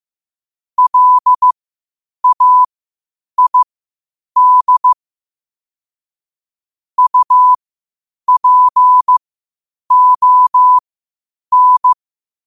Morse code, heard as a keyed tone, about 15 wpm, spaced as for 8 wpm, 1000 Hz. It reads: LAID UPON
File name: qotdfiles / 8wpm